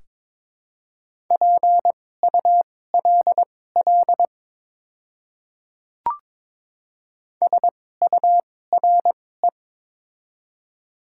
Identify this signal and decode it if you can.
Morse code, heard as a keyed tone, about 22 wpm, spaced as for 16 wpm, 700 Hz. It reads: PULL SURE
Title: 2x Character Spacing - Top 500 Words - 22wpm